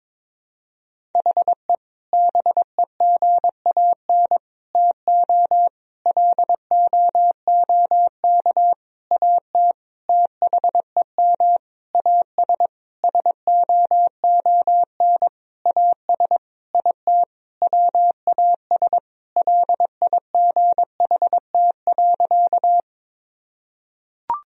Morse code, heard as a keyed tone, about 22 wpm, 700 Hz. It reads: HE BEGAN TO LOOK AT THEM AS SOON AS IT WAS LIGHT.
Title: Sentences from Top 1000 Words - Review Part 1 - 22wpm